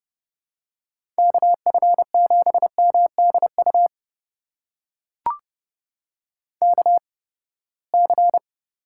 Morse code, heard as a keyed tone, about 30 wpm, spaced as for 21 wpm, 700 Hz. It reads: KF7MDU KC
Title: USA Call Signs - 6 Characters - 8x Character Spacing - 30wpm